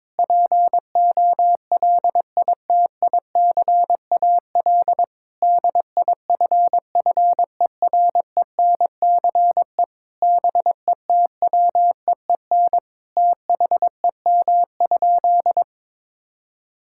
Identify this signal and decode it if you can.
Morse code, heard as a keyed tone, about 22 wpm, 700 Hz. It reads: POLITICAL DIFFERENCE BETWEEN THEM?
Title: Sentences from Top 1000 Words - Review Part 5 - 22wpm